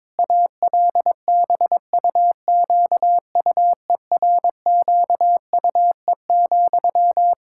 Morse code, heard as a keyed tone, about 22 wpm, 700 Hz. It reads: ALBUQUERQUE,
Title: K1USN SST QSO Practice 22wpm